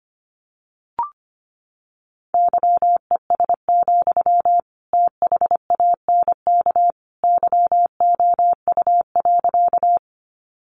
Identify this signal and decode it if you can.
Morse code, heard as a keyed tone, about 25 wpm, 700 Hz. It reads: YES, THANK YOU.
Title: Sentences from Top 600 Words 25wpm